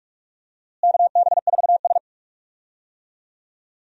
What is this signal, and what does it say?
Morse code, heard as a keyed tone, about 44 wpm, 700 Hz. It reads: KB4S
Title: USA Call Signs - 4 Characters - Repeated 2x - 45wpm